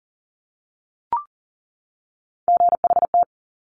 Morse code, heard as a keyed tone, about 38 wpm, 700 Hz. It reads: GHT
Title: Top Three Letter Combinations in Words 40wpm